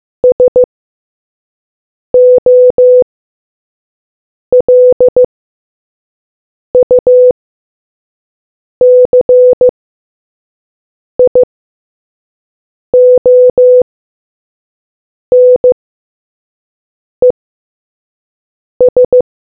Morse code, heard as a keyed tone, about 15 wpm, spaced as for 5 wpm, 500 Hz. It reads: SOLUCIONES